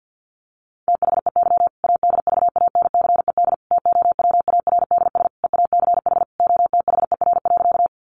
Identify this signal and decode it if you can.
Morse code, heard as a keyed tone, about 50 wpm, 700 Hz. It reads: THEY ADVANCED TOWARDS EACH OTHER.
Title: Sentences from Top 1200 Words 50wpm